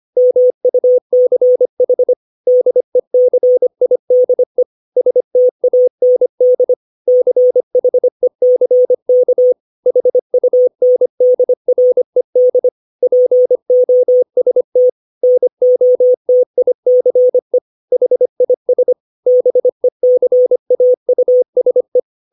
Morse code, heard as a keyed tone, about 25 wpm, 500 Hz. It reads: MUCH DECIDE STAND CHECK HUNDRED POST NOTICE HIS BECAUSE
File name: 632 words / 632 Words 25WPM